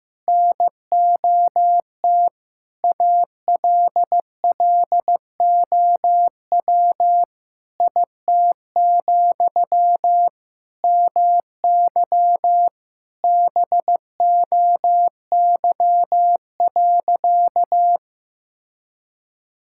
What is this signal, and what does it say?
Morse code, heard as a keyed tone, about 15 wpm, 700 Hz. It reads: NOT ALLOW IT, MY BOY.
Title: Sentences from Top 1000 Words - Review Part 5 - 15wpm